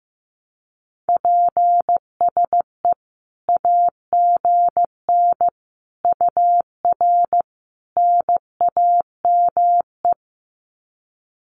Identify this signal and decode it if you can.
Morse code, heard as a keyed tone, about 15 wpm, 700 Hz. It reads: PSE AGN UR NAME
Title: Common QSO Phrases 15wpm